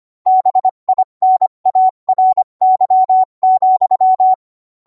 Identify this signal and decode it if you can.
Morse code, heard as a keyed tone, about 25 wpm, 750 Hz. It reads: BINARY,